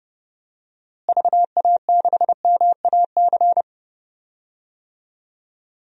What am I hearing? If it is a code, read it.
Morse code, heard as a keyed tone, about 30 wpm, 700 Hz. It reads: VA6MAC
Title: DXCC-100 Call Signs - 6 Characters - Repeated 3x - 30wpm